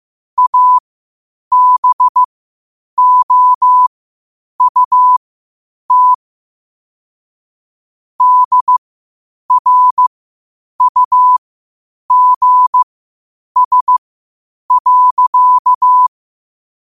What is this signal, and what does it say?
Morse code, heard as a keyed tone, about 15 wpm, spaced as for 8 wpm, 1000 Hz. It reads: ABOUT DRUGS.
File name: qotdfiles / 8wpm